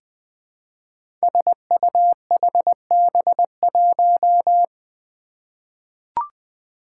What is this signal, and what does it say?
Morse code, heard as a keyed tone, about 20 wpm, 700 Hz. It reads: SUHB1
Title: Five Challenging Letter-Number Combinations 20wpm